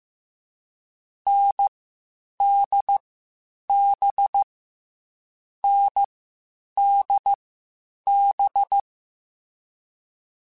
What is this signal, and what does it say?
Morse code, heard as a keyed tone, about 15 wpm, 800 Hz. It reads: N D B N D B
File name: Short Runs / Short Runs-mp3s / NDB NDB6 @13